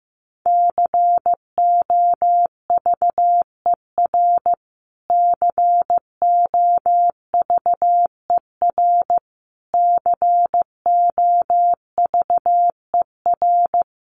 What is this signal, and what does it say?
Morse code, heard as a keyed tone, about 15 wpm, 700 Hz. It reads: COVER COVER COVER
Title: Top 300 Words Repeated 15wpm